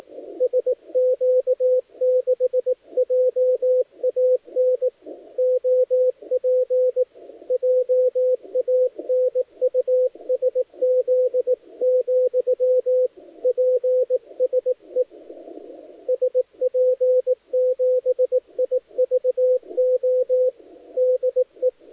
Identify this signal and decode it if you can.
Morse code, heard as a keyed tone, about 18 wpm, 510 Hz, with radio noise. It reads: SQ6JAN OP JANUSZ, PSE SP7IVO DE
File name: dlazaawansowanych / nagraniazpasma / titawaNETcontrol